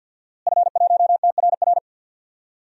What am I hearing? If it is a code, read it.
Morse code, heard as a keyed tone, about 52 wpm, 700 Hz. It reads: F1TRR
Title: DXCC-100 Call Signs - 5 Characters - Repeated 3x - 50wpm